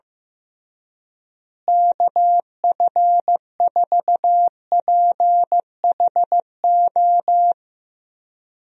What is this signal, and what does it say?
Morse code, heard as a keyed tone, about 15 wpm, 700 Hz. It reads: KF4PHO
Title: USA Call Signs - 6 Characters - Repeated 3x - 15wpm